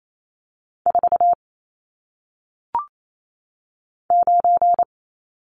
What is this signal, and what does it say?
Morse code, heard as a keyed tone, about 28 wpm, 700 Hz. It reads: 4 9
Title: Numbers - 1-Digit 28wpm